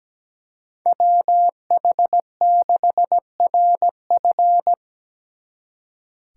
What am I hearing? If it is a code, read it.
Morse code, heard as a keyed tone, about 20 wpm, 700 Hz. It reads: WH6RF
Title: Speed Racing - USA 2x2 Call Signs 17wpm